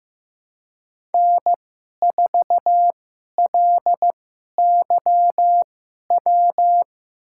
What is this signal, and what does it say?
Morse code, heard as a keyed tone, about 15 wpm, spaced as for 11 wpm, 700 Hz. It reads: N4LYW
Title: USA Call Signs - 5 Characters - 2x Character Spacing - 15wpm